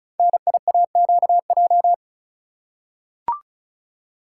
Morse code, heard as a keyed tone, about 36 wpm, 700 Hz. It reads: NIAQJ
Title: Code Groups - 5 Letters - 35wpm